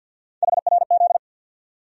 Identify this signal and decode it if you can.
Morse code, heard as a keyed tone, about 47 wpm, 700 Hz. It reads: HRZ
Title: USA Call Sign Postfixes - 3 Letters 50wpm